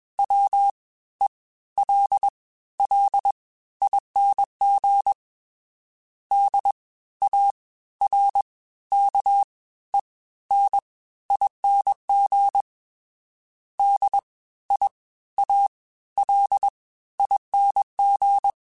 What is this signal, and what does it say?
Morse code, heard as a keyed tone, about 21 wpm, 800 Hz. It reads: W E L L ING D A R K E N ING D I A L ING